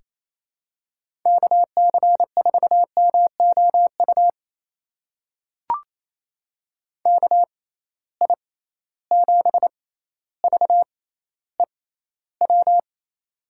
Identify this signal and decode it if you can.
Morse code, heard as a keyed tone, about 28 wpm, spaced as for 14 wpm, 700 Hz. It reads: KC4MOU KI7VEW
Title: USA Call Signs - 6 Characters - 6x Character Spacing - 28wpm